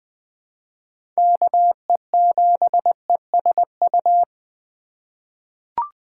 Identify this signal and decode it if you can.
Morse code, heard as a keyed tone, about 20 wpm, 700 Hz. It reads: KE7ESU
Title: USA Call Signs - 6 Characters - 3x Character Spacing - 20wpm